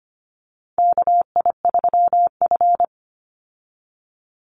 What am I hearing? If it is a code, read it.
Morse code, heard as a keyed tone, about 25 wpm, 700 Hz. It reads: KI3F
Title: Call Signs Repeated 2x - 25wpm